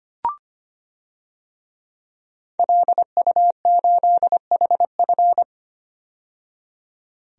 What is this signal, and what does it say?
Morse code, heard as a keyed tone, about 25 wpm, 700 Hz. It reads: LU8HF
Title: DXCC-100 Call Signs - 5 Characters - 25wpm